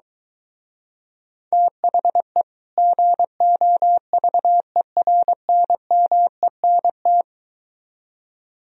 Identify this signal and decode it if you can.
Morse code, heard as a keyed tone, about 23 wpm, 700 Hz. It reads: THE GOVERNMENT